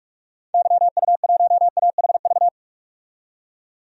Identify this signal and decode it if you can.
Morse code, heard as a keyed tone, about 45 wpm, 700 Hz. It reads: YU1AHV
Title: DXCC-100 Call Signs - 6 Characters - 45wpm